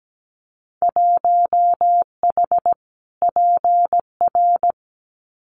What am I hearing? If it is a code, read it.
Morse code, heard as a keyed tone, about 17 wpm, 700 Hz. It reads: 1H PR
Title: Winter Field Day Exchanges Encore 17wpm